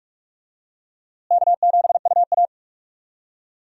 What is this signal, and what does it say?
Morse code, heard as a keyed tone, about 47 wpm, 700 Hz. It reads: K7UA
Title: USA Call Signs - 4 Characters - Repeated 2x - 45wpm